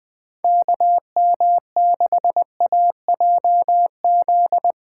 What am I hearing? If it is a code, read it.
Morse code, heard as a keyed tone, about 20 wpm, 700 Hz. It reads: KM6AJZ
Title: USA Call Signs - 6 Characters - 2x Character Spacing - 20wpm